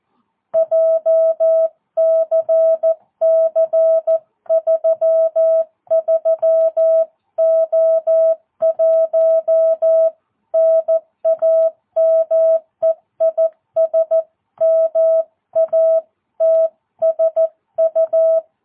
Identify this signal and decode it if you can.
Morse code, heard as a keyed tone, about 14 wpm, 650 Hz. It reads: JCC33O1 NAME IS MATSU
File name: jh8cbh / 2022cw